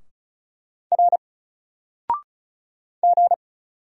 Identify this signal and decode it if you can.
Morse code, heard as a keyed tone, about 34 wpm, 700 Hz. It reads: R G
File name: Single-Letter-Number-35wpm